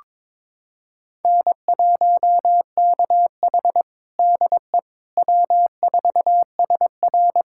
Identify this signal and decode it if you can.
Morse code, heard as a keyed tone, about 22 wpm, 700 Hz. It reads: N1KH DE W4SR
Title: USA Callsign DE Callsign - 4 Characters - 22wpm